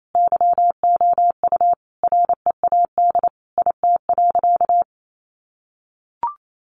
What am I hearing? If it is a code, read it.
Morse code, heard as a keyed tone, about 28 wpm, 700 Hz. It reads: YOU READ IT.
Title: Sentences from Top 1300 Words 28wpm